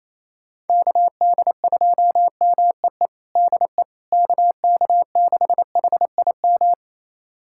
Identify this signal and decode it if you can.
Morse code, heard as a keyed tone, about 28 wpm, 700 Hz. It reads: KD2MEE DE KK6HIM
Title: USA Callsign DE Callsign - 6 Characters - 28wpm